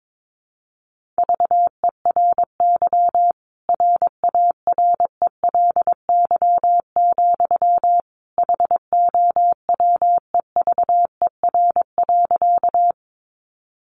Morse code, heard as a keyed tone, about 22 wpm, 700 Hz. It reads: VERY RARELY, HOWEVER.
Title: Sentences from Top 1400 Words 22wpm